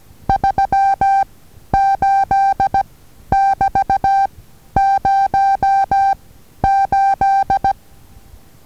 Morse code, heard as a keyed tone, about 16 wpm, 790 Hz. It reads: 3 8 <BT> 0 8
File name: Taviro HA6VA A